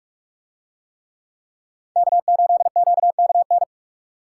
Morse code, heard as a keyed tone, about 45 wpm, 690 Hz. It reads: K8QKN